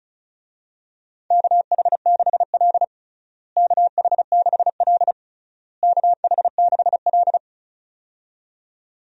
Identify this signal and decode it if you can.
Morse code, heard as a keyed tone, about 34 wpm, 700 Hz. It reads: KH6L KH6L KH6L
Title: DXCC-100 Call Signs - 4 Characters - Repeated 3x - 35wpm